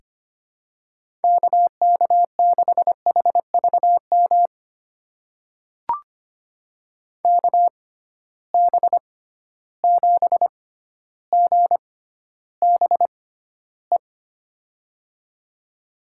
Morse code, heard as a keyed tone, about 25 wpm, spaced as for 13 wpm, 700 Hz. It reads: KK6HVM KB7GBE
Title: USA Call Signs - 6 Characters - 6x Character Spacing - 25wpm